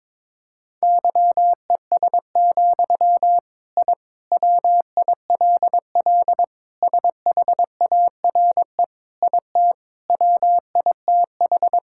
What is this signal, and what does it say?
Morse code, heard as a keyed tone, about 22 wpm, 700 Hz. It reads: YES, I WILL SHARE IT WITH
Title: Sentences from Top 1000 Words - Review Part 6 - 22wpm